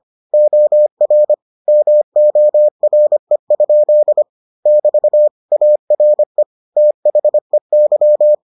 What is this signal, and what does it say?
Morse code, heard as a keyed tone, about 25 wpm, 600 Hz. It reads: OR MORE? <BT> ARE THEY